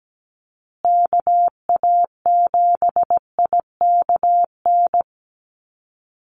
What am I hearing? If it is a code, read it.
Morse code, heard as a keyed tone, about 17 wpm, 700 Hz. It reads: KA7IKN
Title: USA Call Signs Repeated 2x 17wpm